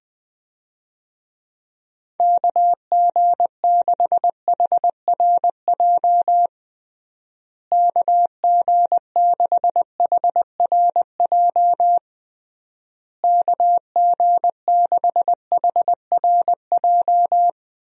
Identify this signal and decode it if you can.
Morse code, heard as a keyed tone, about 20 wpm, 700 Hz. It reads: KG6HRJ KG6HRJ KG6HRJ
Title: USA Call Signs - 6 Characters - Repeated 3x - 20wpm